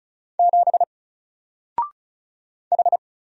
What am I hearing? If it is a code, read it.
Morse code, heard as a keyed tone, about 34 wpm, 700 Hz. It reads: 7 H
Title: Single Letter-Number Alt 35wpm